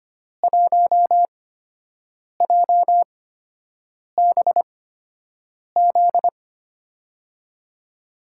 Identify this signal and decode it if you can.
Morse code, heard as a keyed tone, about 25 wpm, spaced as for 7 wpm, 700 Hz. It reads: 1JBZ
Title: USA Call Signs - 6 Characters - 8x Character Spacing - 25wpm